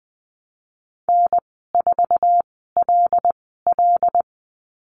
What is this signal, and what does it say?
Morse code, heard as a keyed tone, about 20 wpm, spaced as for 15 wpm, 700 Hz. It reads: N4LL
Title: 154 Lesson 25g - Letter p - Call Signs 20wpm